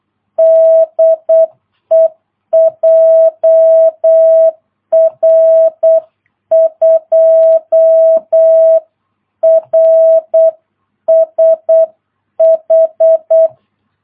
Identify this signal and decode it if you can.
Morse code, heard as a keyed tone, about 8 wpm, 650 Hz. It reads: DE JR2RSH
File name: jh8cbh / 2022cw